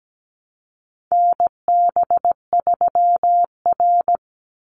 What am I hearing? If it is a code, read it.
Morse code, heard as a keyed tone, about 17 wpm, 700 Hz. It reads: NB3R
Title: Call Signs - 17wpm